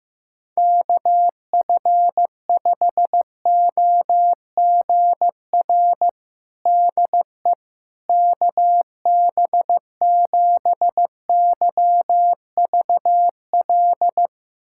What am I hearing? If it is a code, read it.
Morse code, heard as a keyed tone, about 15 wpm, 700 Hz. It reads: KF5OGR DE KB7YVL
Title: USA Callsign DE Callsign - 6 Characters - 15wpm